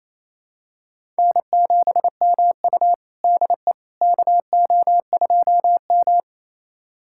Morse code, heard as a keyed tone, about 28 wpm, 700 Hz. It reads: N7MU DE KO2M
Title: USA Callsign DE Callsign - 4 Characters - 28wpm